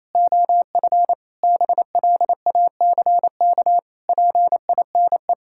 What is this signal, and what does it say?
Morse code, heard as a keyed tone, about 28 wpm, 700 Hz. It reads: OF BLACK PINE
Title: Pangram Challenge v2 - 28wpm